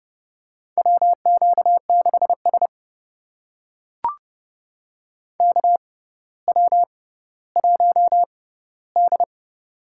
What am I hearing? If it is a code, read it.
Morse code, heard as a keyed tone, about 28 wpm, spaced as for 15 wpm, 700 Hz. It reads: WQ6S KW1D
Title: USA Call Signs - 4 Characters - 6x Character Spacing - 30wpm